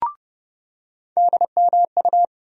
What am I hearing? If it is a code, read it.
Morse code, heard as a keyed tone, about 30 wpm, 700 Hz. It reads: DMU
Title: USA Call Sign Postfixes - 3 Letters 30wpm